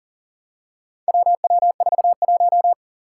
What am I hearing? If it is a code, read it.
Morse code, heard as a keyed tone, about 40 wpm, 700 Hz. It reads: WW41
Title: Four Challenging Letter-Number Combinations 40wpm